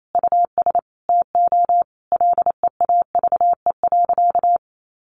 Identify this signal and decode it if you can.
Morse code, heard as a keyed tone, about 28 wpm, 700 Hz. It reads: US TO LEAVE.
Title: Sentences from Top 1500 Words 28wpm